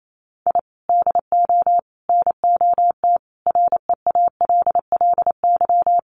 Morse code, heard as a keyed tone, about 28 wpm, 700 Hz. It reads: I DO NOT REALLY
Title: Sets of 5 Words Encore 28wpm